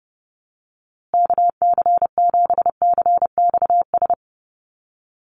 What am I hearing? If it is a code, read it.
Morse code, heard as a keyed tone, about 29 wpm, 700 Hz. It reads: KC7CXS
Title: USA Call Signs Repeated 3x 30wpm